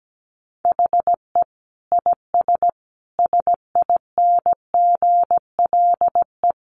Morse code, heard as a keyed tone, about 17 wpm, 700 Hz. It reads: HE IS SINGLE
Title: Sentences from Top 700 Words 17wpm